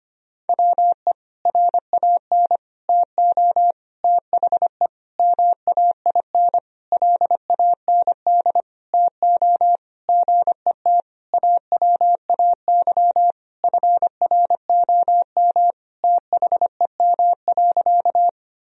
Morse code, heard as a keyed tone, about 25 wpm, 700 Hz. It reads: WE RAN TO THE MAIN LAND TO GET AWAY FROM THEM.
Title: Sentences from Top 1000 Words - Review Part 2 - 25wpm